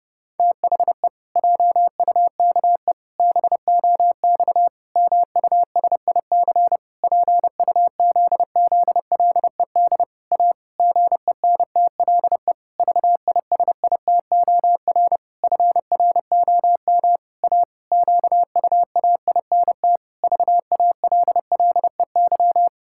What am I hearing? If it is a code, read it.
Morse code, heard as a keyed tone, about 30 wpm, 700 Hz. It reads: THE JUKE BOX MUSIC PUZZLED A GENTLE VISITOR FROM A QUAINT VALLEY
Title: Pangram Challenge v2 - 30wpm